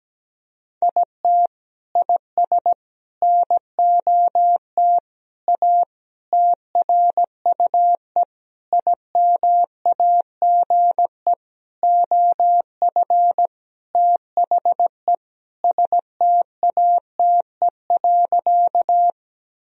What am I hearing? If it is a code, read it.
Morse code, heard as a keyed tone, about 17 wpm, 700 Hz. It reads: IT IS NOT A TRUE IMAGE OF THE STATE.
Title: Sentences from Top 1000 Words - Review Part 4 - 17wpm